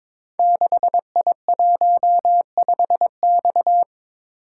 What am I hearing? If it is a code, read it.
Morse code, heard as a keyed tone, about 22 wpm, 700 Hz. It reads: 6I15X
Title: Code Groups - 5 Letter:Numbers - 22wpm